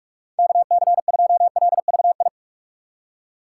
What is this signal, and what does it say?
Morse code, heard as a keyed tone, about 46 wpm, 700 Hz. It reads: KC2LVI
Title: USA Call Signs - 6 Characters - 2x Character Spacing - 45wpm